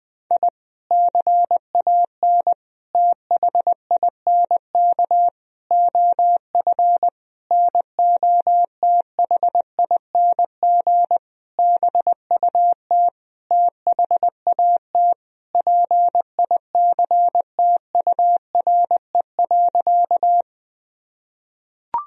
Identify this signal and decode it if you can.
Morse code, heard as a keyed tone, about 20 wpm, 700 Hz. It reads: I CAN THINK OF NOTHING BUT THAT PICTURE.
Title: Sentences from Top 1000 Words - Review Part 1 - 20wpm